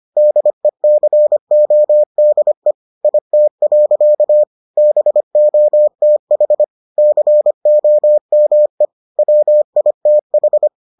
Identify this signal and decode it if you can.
Morse code, heard as a keyed tone, about 25 wpm, 600 Hz. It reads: DECODE IT. BOTH COME WITH